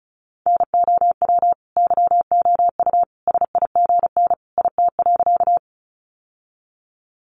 Morse code, heard as a keyed tone, about 34 wpm, 700 Hz. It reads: NOW YOU SIGN IT.
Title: Sentences from Top 1000 Words 35wpm